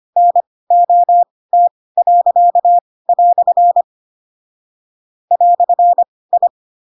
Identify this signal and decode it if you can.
Morse code, heard as a keyed tone, about 25 wpm, spaced as for 18 wpm, 700 Hz. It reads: NOT." "I
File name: Pinocchio / Pinocchio ch2 12 25 ik1hge